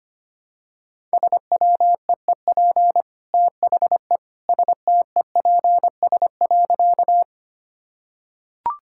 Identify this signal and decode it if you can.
Morse code, heard as a keyed tone, about 25 wpm, 690 Hz. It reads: SWEEP THE STEPS.
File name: Easy Sentences with Hints - 25wpm